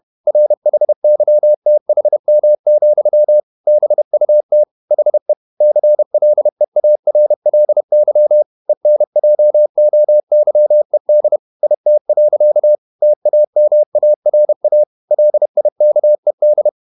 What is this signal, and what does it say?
Morse code, heard as a keyed tone, about 31 wpm, 600 Hz. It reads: RHYTHM, BUT HE CLEARLY ENJOYED IT. TAMARA LIKED
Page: Never 31wpm | CW med Gnister